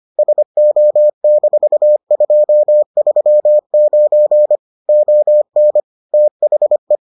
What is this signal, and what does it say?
Morse code, heard as a keyed tone, about 25 wpm, 600 Hz. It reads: SO-239 ON THE